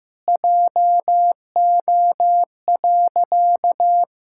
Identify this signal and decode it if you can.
Morse code, heard as a keyed tone, about 15 wpm, 700 Hz. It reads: JO.
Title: Pangram Challenge v2 - 15wpm